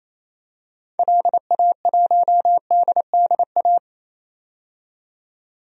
Morse code, heard as a keyed tone, about 28 wpm, 700 Hz. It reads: LA1DDA
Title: DXCC-100 Call Signs - 6 Characters - 28wpm